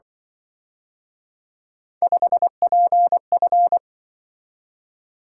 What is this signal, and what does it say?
Morse code, heard as a keyed tone, about 23 wpm, 700 Hz. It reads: 5PF
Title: Speed-Racing - Three Letter-Number Combinations - 20wpm